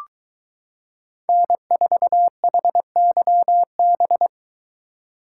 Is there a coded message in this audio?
Yes: Morse code from 1.3 to 4.3 s, heard as a keyed tone, about 23 wpm, 700 Hz. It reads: N4HYB